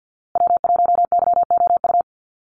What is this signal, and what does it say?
Morse code, heard as a keyed tone, about 50 wpm, 700 Hz. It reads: W1YOU
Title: USA Call Signs Repeated 3x 50wpm